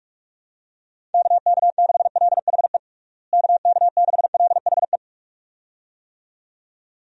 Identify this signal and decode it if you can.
Morse code, heard as a keyed tone, about 46 wpm, 700 Hz. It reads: KK6LHE KK6LHE
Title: USA Call Signs - 6 Characters - Repeated 2x - 45wpm